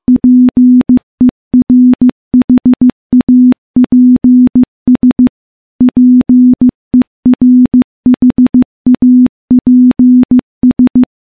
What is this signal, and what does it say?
Morse code, heard as a keyed tone, about 15 wpm, 250 Hz. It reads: PERHAPS PERHAPS
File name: morse / morse-training-15WPM